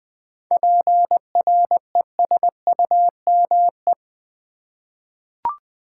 Morse code, heard as a keyed tone, about 20 wpm, 700 Hz. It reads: PRESUME
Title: 7 Letter Words - First 3 Letters Given - 20wpm